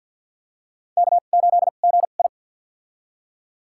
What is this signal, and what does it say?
Morse code, heard as a keyed tone, about 52 wpm, spaced as for 36 wpm, 700 Hz. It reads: K8GI
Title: USA Call Signs - 4 Characters - 2x Character Spacing - 50wpm